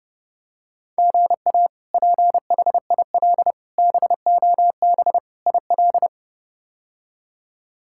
Morse code, heard as a keyed tone, about 32 wpm, 700 Hz. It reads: GA PHIL BOB IL
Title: K1USN SST QSO Practice - 30wpm